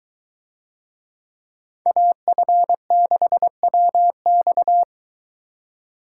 Morse code, heard as a keyed tone, about 22 wpm, 700 Hz. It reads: AF6WX